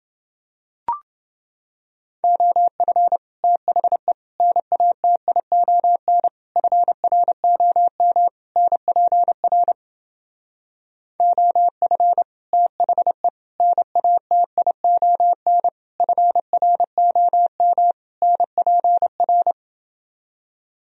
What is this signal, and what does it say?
Morse code, heard as a keyed tone, about 29 wpm, 700 Hz. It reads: OF THE NATION FROM NPR OF THE NATION FROM NPR